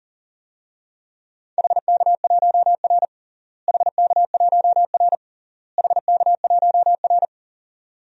Morse code, heard as a keyed tone, about 40 wpm, 700 Hz. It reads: HK1R HK1R HK1R
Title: DXCC-100 Call Signs - 4 Characters - Repeated 3x - 40wpm